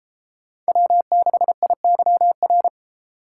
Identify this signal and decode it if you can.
Morse code, heard as a keyed tone, about 32 wpm, 700 Hz. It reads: W6IYR